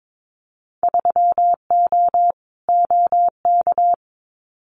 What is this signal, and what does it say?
Morse code, heard as a keyed tone, about 22 wpm, 700 Hz. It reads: 3O OK
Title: Winter Field Day Exchanges 22wpm